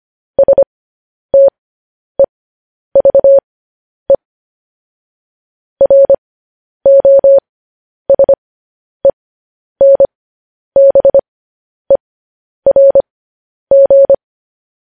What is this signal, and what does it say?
Morse code, heard as a keyed tone, about 25 wpm, spaced as for 10 wpm, 550 Hz. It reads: STEVE ROSENBERG